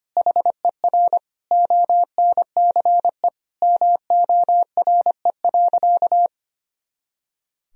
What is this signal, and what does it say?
Morse code, heard as a keyed tone, about 25 wpm, 700 Hz. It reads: HER ONCE MORE.
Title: Sentences from Top 500 Words and Abbreviations 25wpm